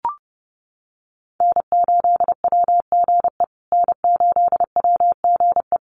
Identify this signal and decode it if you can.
Morse code, heard as a keyed tone, about 30 wpm, 700 Hz. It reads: N8WGE N8WGE
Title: USA Call Signs Repeated 3x 30wpm